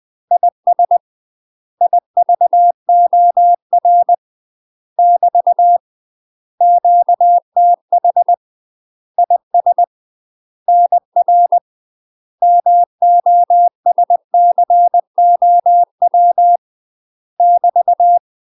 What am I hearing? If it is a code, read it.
Morse code, heard as a keyed tone, about 20 wpm, 700 Hz. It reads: IS IVOR <BT> QTH IS NR MOSCOW <BT>
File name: UK MASTER QSOs TRAINING / 20WPM / UK MASTER QSOs 0041